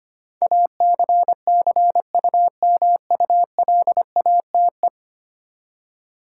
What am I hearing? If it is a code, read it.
Morse code, heard as a keyed tone, about 25 wpm, 700 Hz. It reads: ACCUMULATE
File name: Ten Letter Words - 25wpm